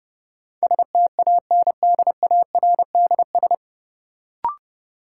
Morse code, heard as a keyed tone, about 32 wpm, 700 Hz. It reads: STANDARDS
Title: Sentences from Top 1000 Words - Review Part 6 - New Word Drill - 30wpm